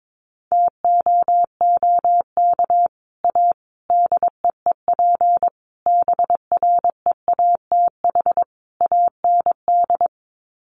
Morse code, heard as a keyed tone, about 22 wpm, 700 Hz. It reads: TOOK A DEEP BREATH AND
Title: Sets of 5 Words Encore 22wpm